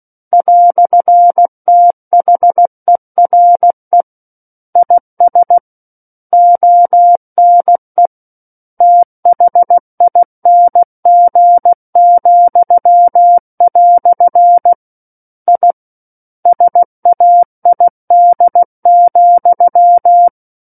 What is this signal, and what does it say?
Morse code, heard as a keyed tone, about 16 wpm, 700 Hz. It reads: "THERE IS ONE THING," I SAID,
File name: Book One / 07-Chapter 7 - 16 WPM